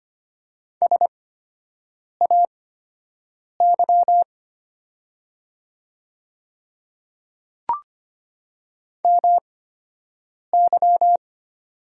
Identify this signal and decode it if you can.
Morse code, heard as a keyed tone, about 25 wpm, spaced as for 7 wpm, 700 Hz. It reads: SAY MY
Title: 8x Character Spacing - Top 200 Words - 25wpm